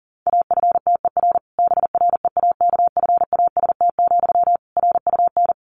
Morse code, heard as a keyed tone, about 40 wpm, 700 Hz. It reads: AFTER BREAKFAST, RUN
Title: Sentences from Top 1500 Words 40wpm